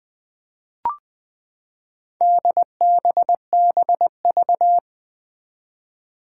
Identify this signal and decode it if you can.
Morse code, heard as a keyed tone, about 20 wpm, 700 Hz. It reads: DBBV
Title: Four Challenging Letter-Number Combinations 20wpm